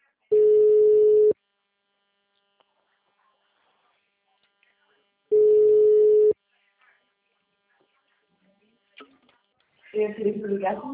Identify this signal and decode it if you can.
Ringback tone (Europe, 425 Hz), with speech.